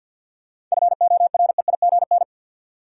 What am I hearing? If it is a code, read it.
Morse code, heard as a keyed tone, about 53 wpm, 700 Hz. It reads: FOREIGN
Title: Seven Letter Words - 50wpm